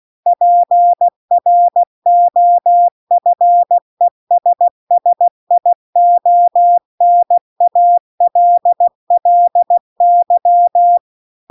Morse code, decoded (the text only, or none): PROFESSIONALLY